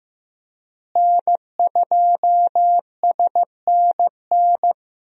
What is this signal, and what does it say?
Morse code, heard as a keyed tone, about 16 wpm, 700 Hz. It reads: N2SNN
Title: USA Call Signs - 5 Characters - Repeated 3x - 15wpm